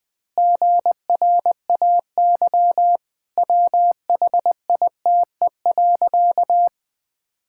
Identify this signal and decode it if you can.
Morse code, heard as a keyed tone, about 19 wpm, 700 Hz. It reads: GRAY WHITE.
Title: Pangram Challenge v2 - 20wpm